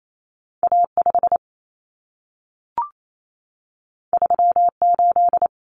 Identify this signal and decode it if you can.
Morse code, heard as a keyed tone, about 28 wpm, 700 Hz. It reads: A5 38
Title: Common Cut Numbers - 2-Digit 28wpm